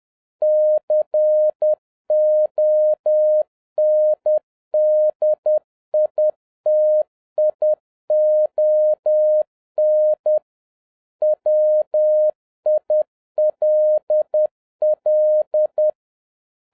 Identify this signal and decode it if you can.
Morse code, heard as a keyed tone, about 10 wpm, 610 Hz. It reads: CONDITION WILL